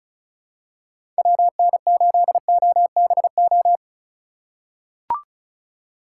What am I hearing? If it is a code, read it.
Morse code, heard as a keyed tone, about 36 wpm, 700 Hz. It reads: WN8OBO
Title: USA Call Signs - 6 Characters - 2x Character Spacing - 35wpm